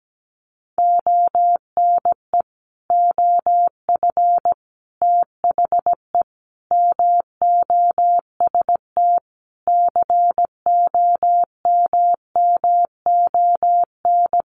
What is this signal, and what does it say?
Morse code, heard as a keyed tone, about 17 wpm, 700 Hz. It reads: ONE OF THE MOST COMMON
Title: Sets of 5 Words Encore2 17wpm